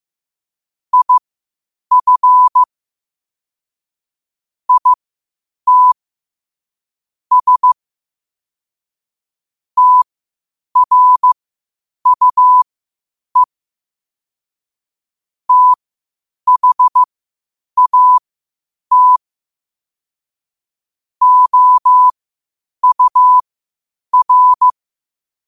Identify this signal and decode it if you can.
Morse code, heard as a keyed tone, about 15 wpm, spaced as for 8 wpm, 1000 Hz. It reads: IF IT S TRUE THAT OUR
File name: qotdfiles / 8wpm